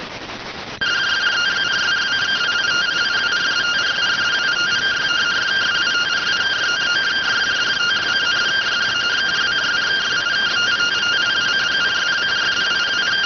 Two-tone teleprinter signal, about 170 Hz shift.